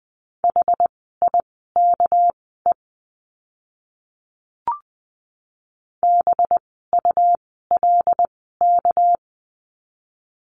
Morse code, heard as a keyed tone, about 19 wpm, spaced as for 14 wpm, 700 Hz. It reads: HIKE BULK
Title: 199 Lesson 32f - Letter k - Words 20wpm